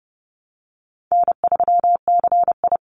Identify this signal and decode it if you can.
Morse code, heard as a keyed tone, about 30 wpm, 700 Hz. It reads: N3CI
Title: Easy Call Signs Repeated 2x - 30wpm